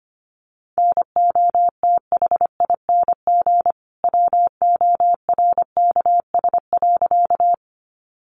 Morse code, heard as a keyed tone, about 25 wpm, 700 Hz. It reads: NOTHING WORKS.
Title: Additional Easy Phrases 25wpm